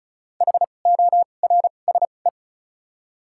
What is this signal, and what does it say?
Morse code, heard as a keyed tone, about 36 wpm, spaced as for 25 wpm, 700 Hz. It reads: HORSE
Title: 2x Character Spacing - Top 300 Words - 35wpm